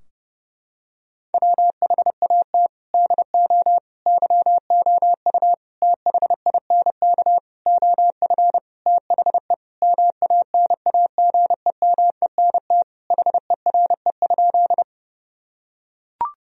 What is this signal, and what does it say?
Morse code, heard as a keyed tone, about 30 wpm, 700 Hz. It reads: WHAT DO YOU THINK OF THE MANAGEMENT HERE?
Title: Sentences from Top 1000 Words - Review Part 4 - 30wpm